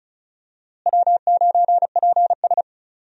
Morse code, heard as a keyed tone, about 36 wpm, 700 Hz. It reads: W9PS
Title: USA Call Signs - 4 Characters - Repeated 3x - 35wpm